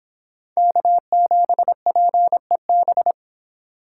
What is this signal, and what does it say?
Morse code, heard as a keyed tone, about 26 wpm, 700 Hz. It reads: K7PEB